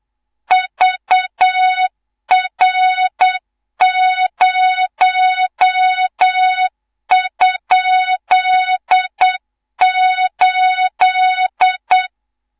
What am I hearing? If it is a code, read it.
Morse code, heard as a keyed tone, about 8 wpm, 3700 Hz, with noise. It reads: VR0?8
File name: Morze-HA2MN / hang / 18 IRASJELEK2